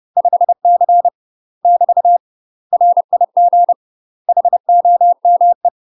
Morse code, heard as a keyed tone, about 30 wpm, 700 Hz. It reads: 5C <BT> RIG HOME
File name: UK MASTER QSOs TRAINING / 30WPM / UK MASTER QSOs 0016